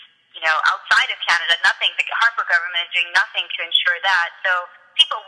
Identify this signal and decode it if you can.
Emergency Alert System (SAME) tones, heard as harsh buzzing bursts, with speech. It is heard from 0.4 to 4.5 s.